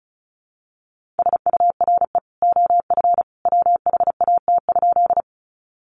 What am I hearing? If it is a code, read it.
Morse code, heard as a keyed tone, about 34 wpm, 700 Hz. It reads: SURE OF WHAT?
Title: Sentences from Top 300 Words 35wpm